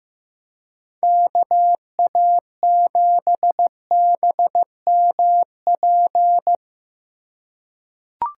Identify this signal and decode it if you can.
Morse code, heard as a keyed tone, about 15 wpm, 700 Hz. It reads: KA7BMP
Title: USA Call Signs - 6 Characters - 3x Character Spacing - 15wpm